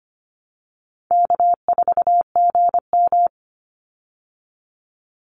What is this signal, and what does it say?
Morse code, heard as a keyed tone, about 25 wpm, 700 Hz. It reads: K4GM
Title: Easy Call Signs Repeated 3x - 25wpm